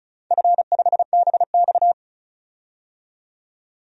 Morse code, heard as a keyed tone, about 34 wpm, 700 Hz. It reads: F5BX
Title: DXCC-100 Call Signs - 4 Characters - 35wpm